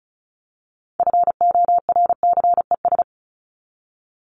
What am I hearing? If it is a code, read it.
Morse code, heard as a keyed tone, about 34 wpm, 700 Hz. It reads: FORCES
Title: ES:S Suffix - Top 100 Words with Base Words 35wpm